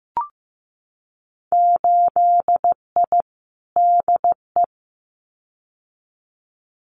Morse code, heard as a keyed tone, about 15 wpm, 700 Hz. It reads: 8I DE
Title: Winter Field Day Exchanges Encore 15wpm